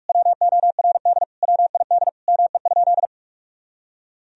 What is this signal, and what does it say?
Morse code, heard as a keyed tone, about 46 wpm, 700 Hz. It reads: WORD WID ME?
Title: Sentences from Top 300 Words and Abbreviations 45wpm